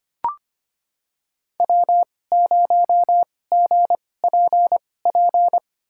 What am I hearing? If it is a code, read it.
Morse code, heard as a keyed tone, about 25 wpm, spaced as for 18 wpm, 700 Hz. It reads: W0GPP
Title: USA Call Signs - 5 Characters - 2x Character Spacing - 25wpm